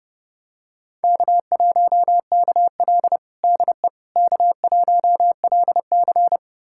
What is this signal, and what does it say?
Morse code, heard as a keyed tone, about 32 wpm, 700 Hz. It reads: K1KL DE K1LC
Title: USA Callsign DE Callsign - 4 Characters - 30wpm